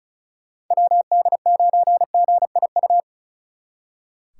Morse code, heard as a keyed tone, about 43 wpm, 700 Hz. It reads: WD9GIU
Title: Speed Racing - USA 2x3 Call Signs 35wpm